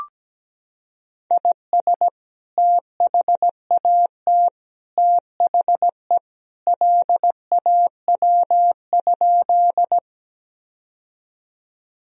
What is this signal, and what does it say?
Morse code, heard as a keyed tone, about 17 wpm, 700 Hz. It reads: IS THAT THE LAW?
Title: Sentences from Top 1000 Words - Review Part 3 - 17wpm